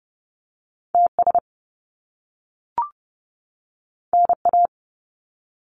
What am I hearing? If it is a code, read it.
Morse code, heard as a keyed tone, about 30 wpm, 700 Hz. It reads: TS NA
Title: Top Two Letter Combinations in Words 30wpm